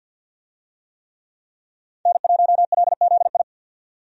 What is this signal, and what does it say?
Morse code, heard as a keyed tone, about 50 wpm, 700 Hz. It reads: N1LZI
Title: USA Call Signs - 5 Characters - 50wpm